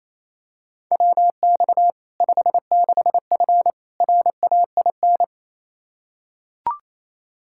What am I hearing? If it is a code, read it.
Morse code, heard as a keyed tone, about 28 wpm, 700 Hz. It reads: WX 56F RAIN
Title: Practice QSO - Exchange 2 - Info - 28wpm